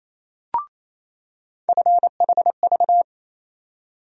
Morse code, heard as a keyed tone, about 28 wpm, 700 Hz. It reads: FHV
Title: USA Call Sign Postfixes - 3 Letters 28wpm